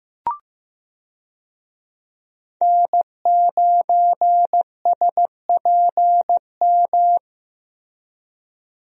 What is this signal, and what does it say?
Morse code, heard as a keyed tone, about 15 wpm, 700 Hz. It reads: N9SPM
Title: USA Call Signs - 5 Characters - Repeated 2x - 15wpm